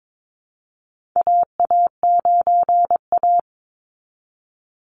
Morse code, heard as a keyed tone, about 22 wpm, 700 Hz. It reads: AA9A
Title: Call Signs - 22wpm